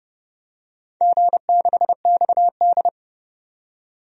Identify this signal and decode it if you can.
Morse code, heard as a keyed tone, about 30 wpm, 700 Hz. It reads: G6XD
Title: DXCC-100 Call Signs - 4 Characters - 30wpm